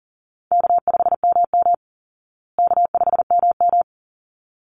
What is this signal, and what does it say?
Morse code, heard as a keyed tone, about 40 wpm, 700 Hz. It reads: K5MM K5MM
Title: Easy Call Signs Repeated 2x - 40wpm